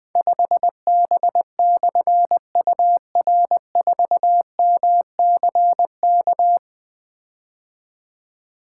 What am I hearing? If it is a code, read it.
Morse code, heard as a keyed tone, about 21 wpm, 700 Hz. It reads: 5B/UR4MCK
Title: Difficult Call Signs 20wpm